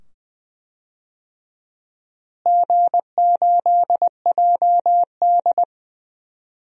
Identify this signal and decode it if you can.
Morse code, heard as a keyed tone, about 20 wpm, 700 Hz. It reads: G8JD
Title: DXCC-100 Call Signs - 4 Characters - Repeated 2x - 20wpm